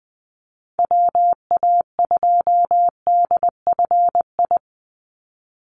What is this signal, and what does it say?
Morse code, heard as a keyed tone, about 20 wpm, 700 Hz. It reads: WA2DFI
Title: Call Signs Repeated 2x - 20wpm